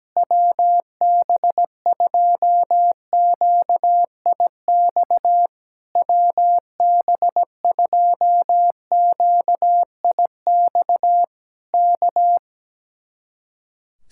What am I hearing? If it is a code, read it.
Morse code, heard as a keyed tone, about 17 wpm, 700 Hz. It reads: WB2QIX WB2QIX K
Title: Speed-Racing Common QSO Phrases - 17wpm